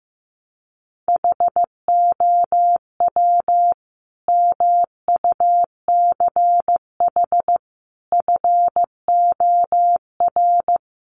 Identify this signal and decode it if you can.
Morse code, heard as a keyed tone, about 15 wpm, 700 Hz. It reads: HOW MUCH FOR
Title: Sentences from Top 1200 Words 15wpm